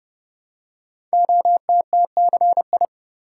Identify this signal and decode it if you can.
Morse code, heard as a keyed tone, about 30 wpm, 700 Hz. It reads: OTTCI
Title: Code Groups - 5 Letters - 30wpm